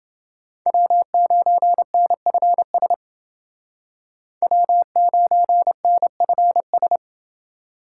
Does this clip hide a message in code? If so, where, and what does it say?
Yes: Morse code from 0.7 to 7.0 s, heard as a keyed tone, about 28 wpm, 700 Hz. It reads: W9NFS W9NFS